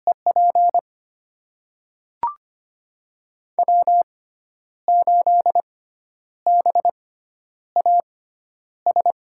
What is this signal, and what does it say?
Morse code, heard as a keyed tone, about 25 wpm, spaced as for 10 wpm, 700 Hz. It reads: EP W8BAS
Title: USA Call Signs - 5 Characters - 6x Character Spacing - 25wpm